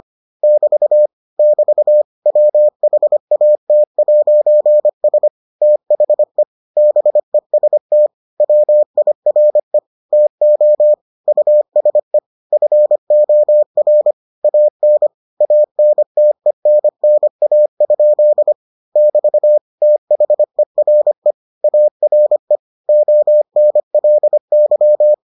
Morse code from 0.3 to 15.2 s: <BT> <BT> WHAT'S THE BEST WIRE TO USE FOR AN

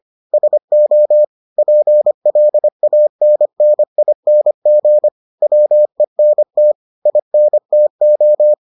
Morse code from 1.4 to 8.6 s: PLANNING WENT INTO

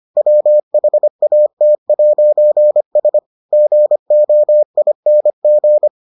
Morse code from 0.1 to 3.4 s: WHAT'S

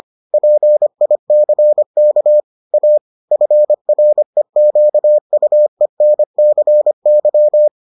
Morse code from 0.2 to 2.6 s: PICK

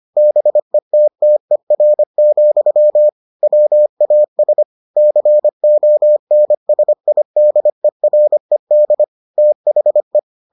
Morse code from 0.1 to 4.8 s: BETTER, WAS

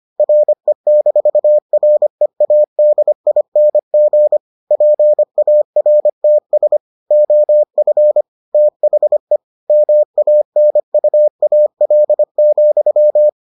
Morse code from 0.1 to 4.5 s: RE-READING

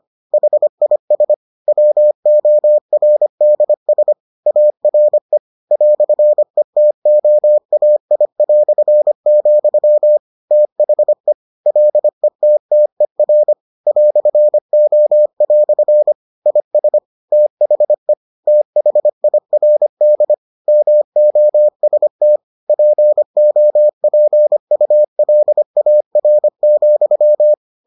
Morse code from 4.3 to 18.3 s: ARE "ETOAI", THE LETTER "O" IS THE

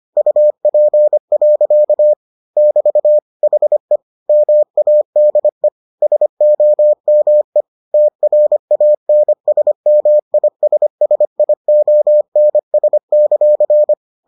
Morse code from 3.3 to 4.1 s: HE

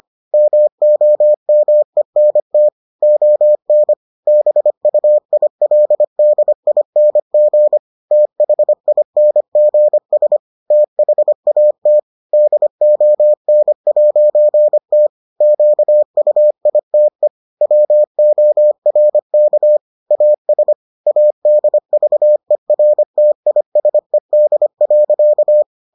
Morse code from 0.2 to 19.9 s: MOMENT ON BUILDING THINGS THAT DON'T QUITE WORK

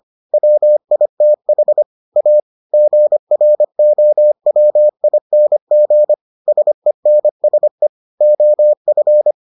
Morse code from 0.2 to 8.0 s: WITH A GROWING SENSE

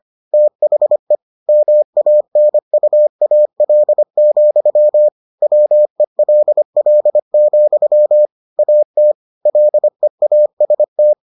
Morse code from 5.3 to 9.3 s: WELL, AT

